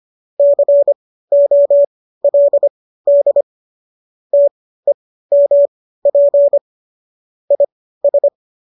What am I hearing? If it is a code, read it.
Morse code, heard as a keyed tone, about 25 wpm, spaced as for 15 wpm, 570 Hz. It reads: COLD TEMP IS
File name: qso203 15